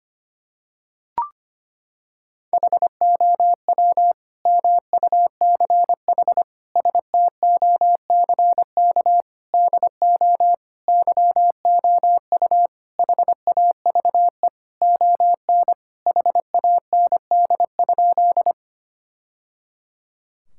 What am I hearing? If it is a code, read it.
Morse code, heard as a keyed tone, about 25 wpm, 700 Hz. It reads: HOW MUCH STOCK DO YOU HAVE ON HAND?
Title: Sentences from Top 1000 Words - Review Part 5 - 25wpm